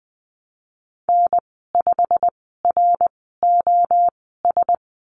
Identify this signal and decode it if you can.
Morse code, heard as a keyed tone, about 19 wpm, spaced as for 14 wpm, 700 Hz. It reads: N5ROS
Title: 088 Lesson 15g - Number 5 - Call Signs 20wpm